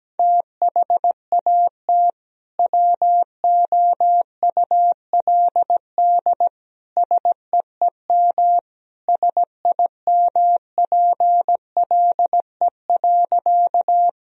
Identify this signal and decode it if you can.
Morse code, heard as a keyed tone, about 17 wpm, 700 Hz. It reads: THAT WOULD SEEM SIMPLE.
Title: Sentences from Top 1000 Words - Review Part 1 - 17wpm